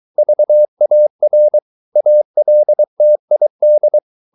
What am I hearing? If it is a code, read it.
Morse code, heard as a keyed tone, about 23 wpm, 600 Hz. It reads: VAR ALTID
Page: Ferie 23 wpm